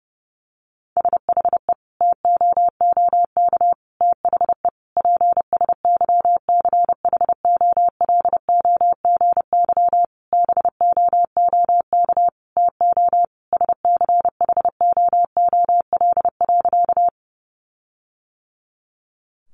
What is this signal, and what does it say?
Morse code, heard as a keyed tone, about 30 wpm, 700 Hz. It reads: SHE TOOK THE PSYCHOLOGY BOOK TO SCHOOL.
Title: Sentences from Top 1400 Words 30wpm